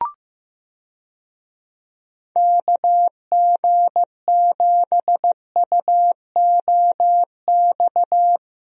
Morse code, heard as a keyed tone, about 15 wpm, 700 Hz. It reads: KG7UOX
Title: USA Call Signs - 6 Characters - Repeated 2x - 15wpm